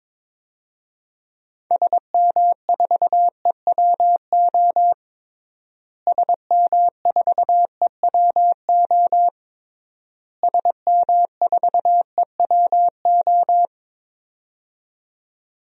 Morse code, heard as a keyed tone, about 22 wpm, 700 Hz. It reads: SM4EWO SM4EWO SM4EWO
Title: DXCC-100 Call Signs - 6 Characters - Repeated 3x - 22wpm